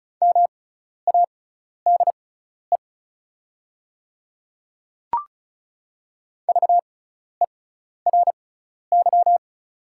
Morse code, heard as a keyed tone, about 36 wpm, spaced as for 12 wpm, 690 Hz. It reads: MADE VERY
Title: 6x Character Spacing - Top 200 Words - 35wpm